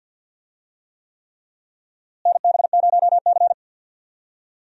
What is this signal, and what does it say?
Morse code, heard as a keyed tone, about 52 wpm, 700 Hz. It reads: NB0C
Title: USA Call Signs - 4 Characters - Repeated 3x - 50wpm